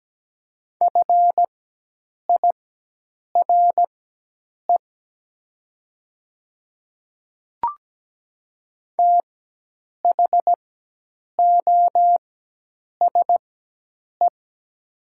Morse code, heard as a keyed tone, about 17 wpm, spaced as for 8 wpm, 690 Hz. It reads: FIRE THOSE
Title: 4x Character Spacing - Top 500 Words - 17wpm